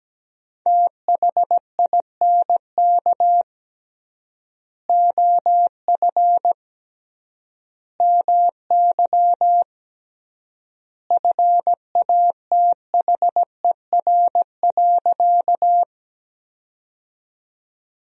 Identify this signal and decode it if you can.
Morse code, heard as a keyed tone, about 17 wpm, 700 Hz. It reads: THINK OF MY FATHER.
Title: 3x Word-Spacing - Sentences from Top 300 Words - 17wpm